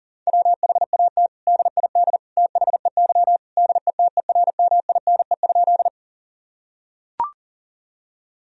Morse code, heard as a keyed tone, about 40 wpm, 700 Hz. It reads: WHAT DID THEY DETERMINE?
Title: Sentences from Top 1000 Words - Review Part 4 - 40wpm